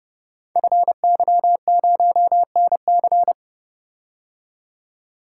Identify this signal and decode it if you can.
Morse code, heard as a keyed tone, about 30 wpm, 700 Hz. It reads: FY0NC
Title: Code Groups - 5 Letter:Numbers - 30wpm